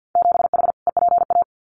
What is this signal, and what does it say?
Morse code, heard as a keyed tone, about 50 wpm, 700 Hz. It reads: 7H EPA
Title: Winter Field Day Exchanges Encore 50wpm